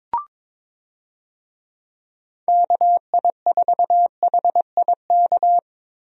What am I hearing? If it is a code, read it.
Morse code, heard as a keyed tone, about 22 wpm, 700 Hz. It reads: KI4HIK
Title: USA Call Signs - 6 Characters - Repeated 3x - 22wpm